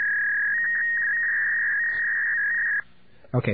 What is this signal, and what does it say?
Fax machine handshake, with speech.